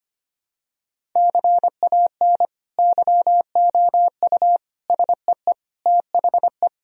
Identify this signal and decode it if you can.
Morse code, heard as a keyed tone, about 25 wpm, 700 Hz. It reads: CAN YOU SEE THE
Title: Sentences from Top 1000 Words - Review Part 1 - 25wpm